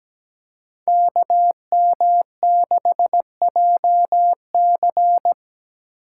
Morse code, heard as a keyed tone, about 17 wpm, 700 Hz. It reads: KM6JC
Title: Code Groups - 5 Letter:Numbers - 17wpm